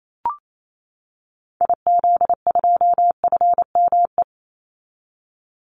Morse code, heard as a keyed tone, about 28 wpm, 700 Hz. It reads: IZ2FME
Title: Call Signs - 28wpm